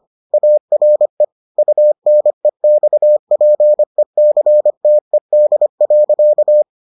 Morse code, heard as a keyed tone, about 25 wpm, 600 Hz. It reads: ARE UNEXPECTED.